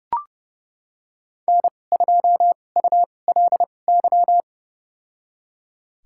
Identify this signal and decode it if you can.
Morse code, heard as a keyed tone, about 29 wpm, spaced as for 22 wpm, 700 Hz. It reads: N2ULY
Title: USA Call Signs - 5 Characters - 2x Character Spacing - 30wpm